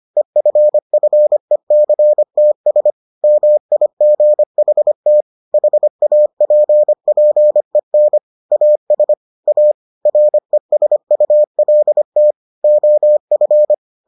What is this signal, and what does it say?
Morse code, heard as a keyed tone, about 25 wpm, 600 Hz. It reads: EFFECTS MIGHT HAPPEN AS A RESULT OF